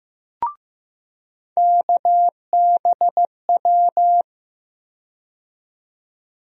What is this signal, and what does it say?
Morse code, heard as a keyed tone, about 16 wpm, 700 Hz. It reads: KBW
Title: USA Call Sign Postfixes - 3 Letters 15wpm